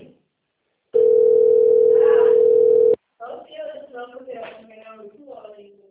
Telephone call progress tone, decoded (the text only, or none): ringback tone